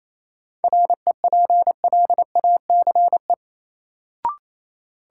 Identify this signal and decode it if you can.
Morse code, heard as a keyed tone, about 28 wpm, 700 Hz. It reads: REPLACE
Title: Sentences from Top 1400 Words - New Word Drill Alt - 28wpm